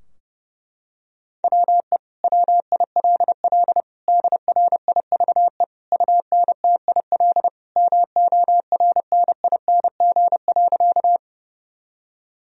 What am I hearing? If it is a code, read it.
Morse code, heard as a keyed tone, about 30 wpm, 700 Hz. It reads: WE WILL DRIVE UNTIL MORNING.
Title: Sentences from Top 1000 Words - Review Part 2 - 30wpm